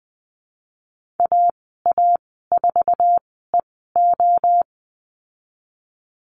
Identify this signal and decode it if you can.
Morse code, heard as a keyed tone, about 19 wpm, spaced as for 14 wpm, 700 Hz. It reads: AA4EO
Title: 055 Lesson 10g - Letter r - Call Signs 20wpm